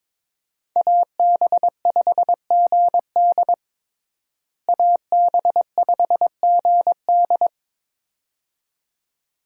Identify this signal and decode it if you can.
Morse code, heard as a keyed tone, about 22 wpm, 700 Hz. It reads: AB5GD AB5GD
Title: USA Call Signs - 5 Characters - Repeated 2x - 22wpm